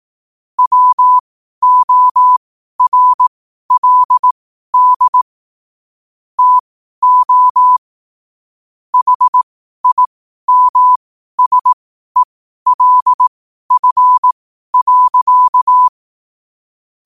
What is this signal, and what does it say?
Morse code, heard as a keyed tone, about 18 wpm, spaced as for 12 wpm, 1000 Hz. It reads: WORLD TO HIMSELF.
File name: qotdfiles / 12wpm